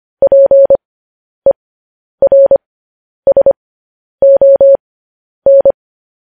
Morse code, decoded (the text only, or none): PERSON